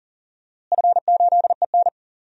Morse code, heard as a keyed tone, about 40 wpm, 700 Hz. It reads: F8EN
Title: DXCC-100 Call Signs - 4 Characters - Repeated 2x - 40wpm